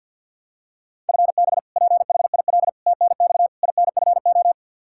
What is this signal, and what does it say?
Morse code, heard as a keyed tone, about 52 wpm, 700 Hz. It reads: FB PHIL TNX INFO
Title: Practice QSO - Exchange 3 - More Info - 50wpm